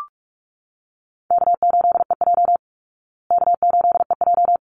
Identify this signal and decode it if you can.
Morse code, heard as a keyed tone, about 44 wpm, 700 Hz. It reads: K8EJ K8EJ
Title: Call Signs Repeated 3x - 45wpm